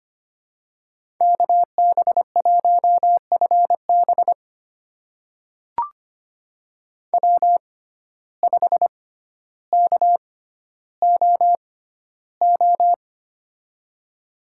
Morse code, heard as a keyed tone, about 25 wpm, spaced as for 13 wpm, 700 Hz. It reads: KB1FB W5KOO
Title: USA Call Signs - 5 Characters - 6x Character Spacing - 25wpm